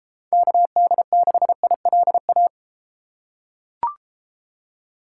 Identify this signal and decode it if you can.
Morse code, heard as a keyed tone, about 32 wpm, 700 Hz. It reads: KD6ILA